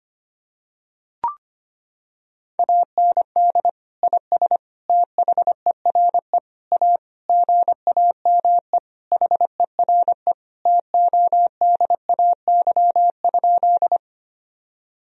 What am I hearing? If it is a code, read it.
Morse code, heard as a keyed tone, about 25 wpm, 700 Hz. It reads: AND IS THERE A GAME HERE TODAY?
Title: Sentences from Top 1000 Words - Review Part 3 - 25wpm